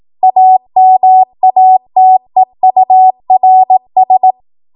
Morse code, heard as a keyed tone, about 18 wpm, 750 Hz. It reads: AMATEURS
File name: Morse / 18WPM